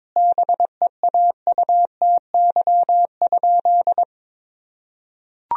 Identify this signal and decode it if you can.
Morse code, heard as a keyed tone, about 22 wpm, 700 Hz. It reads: BEAUTY?
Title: Sentences from Top 1000 Words - Review Part 2 - 22wpm